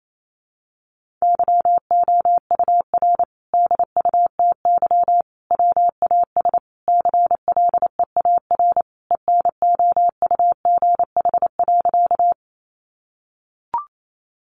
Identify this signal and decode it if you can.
Morse code, heard as a keyed tone, about 28 wpm, 700 Hz. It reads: YOUR DUTY WAS CLEAR ENOUGH.
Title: Sentences from Top 1500 Words 28wpm